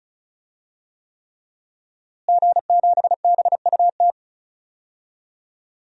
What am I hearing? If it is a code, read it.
Morse code, heard as a keyed tone, about 34 wpm, 700 Hz. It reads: G7BUT
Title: DXCC-100 Call Signs - 5 Characters - Repeated 3x - 35wpm